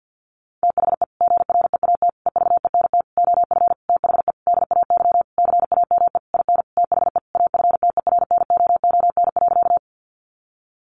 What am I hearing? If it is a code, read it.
Morse code, heard as a keyed tone, about 50 wpm, 700 Hz. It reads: THE GREAT EVENT OF THE DAY CAME IN THE AFTERNOON.
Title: Sentences from Top 1500 Words 50wpm